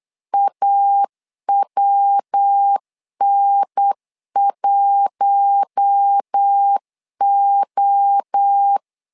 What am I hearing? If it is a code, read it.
Morse code, heard as a keyed tone, about 8 wpm, 800 Hz. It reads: AWN1O
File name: LEZIONI / 4 Q1 8wpm 3035